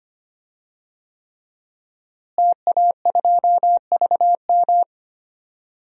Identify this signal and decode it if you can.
Morse code, heard as a keyed tone, about 25 wpm, 700 Hz. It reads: TA2VM
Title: DXCC-100 Call Signs - 5 Characters - Repeated 3x - 25wpm